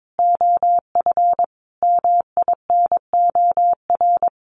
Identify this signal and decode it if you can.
Morse code, heard as a keyed tone, about 22 wpm, 700 Hz. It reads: OF MINOR
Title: Sentences from Top 1500 Words 22wpm